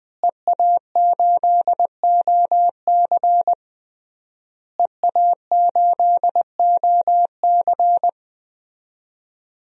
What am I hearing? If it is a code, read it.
Morse code, heard as a keyed tone, about 20 wpm, 700 Hz. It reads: EA8OC EA8OC
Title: DXCC-100 Call Signs - 5 Characters - Repeated 3x - 20wpm